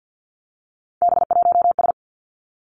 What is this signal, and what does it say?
Morse code, heard as a keyed tone, about 50 wpm, 700 Hz. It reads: B1S
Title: Three Challenging Letter-Number Combinations 50wpm